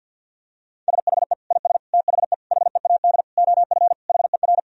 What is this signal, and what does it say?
Morse code, heard as a keyed tone, about 47 wpm, 700 Hz. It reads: SHE IS THE HEAD OF HER
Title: Sentences from Top 1000 Words - Review Part 2 - 50wpm